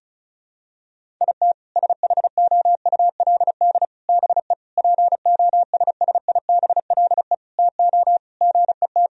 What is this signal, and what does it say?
Morse code, heard as a keyed tone, about 34 wpm, 700 Hz. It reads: IT SHOULD BE POSSIBLE TO GET
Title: Sentences from Top 1000 Words - Review Part 4 - 35wpm